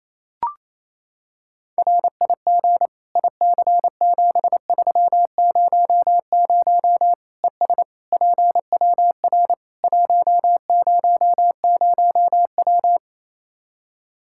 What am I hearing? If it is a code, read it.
Morse code, heard as a keyed tone, about 28 wpm, 700 Hz. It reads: RIG IC7300 ES PWR 100W
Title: Practice QSO - Exchange 2 - Info - 28wpm